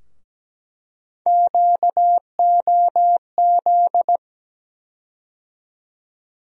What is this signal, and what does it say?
Morse code, heard as a keyed tone, about 17 wpm, 700 Hz. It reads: QOZ
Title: USA Call Sign Postfixes - 3 Letters 17wpm